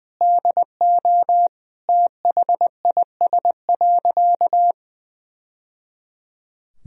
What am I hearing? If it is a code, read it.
Morse code, heard as a keyed tone, about 20 wpm, 700 Hz. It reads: DO THIS.
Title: Sentences from Top 1000 Words - Review Part 2 - 20wpm